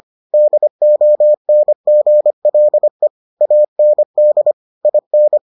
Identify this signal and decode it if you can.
Morse code, heard as a keyed tone, about 25 wpm, 600 Hz. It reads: DONGLE AND IN